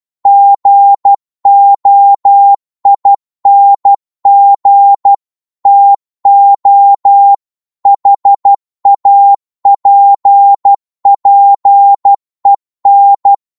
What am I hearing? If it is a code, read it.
Morse code, heard as a keyed tone, about 12 wpm, 800 Hz. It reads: GOING TO HAPPEN
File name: cwmp3 / psg-1-12wpm